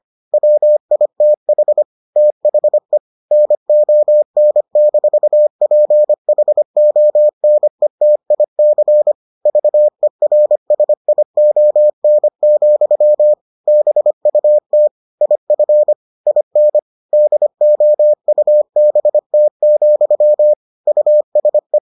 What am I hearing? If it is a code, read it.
Morse code, heard as a keyed tone, about 25 wpm, 600 Hz. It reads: WITH THE NON-PHONETIC VERSION, BUT IF IN DOUBT, USE